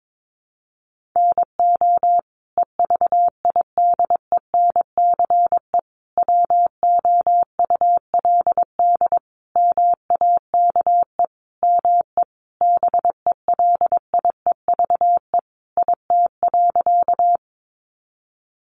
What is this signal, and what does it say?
Morse code, heard as a keyed tone, about 22 wpm, 700 Hz. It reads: NO EVIDENCE WOULD MAKE ME BELIEVE IT.
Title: Sentences from Top 1300 Words 22wpm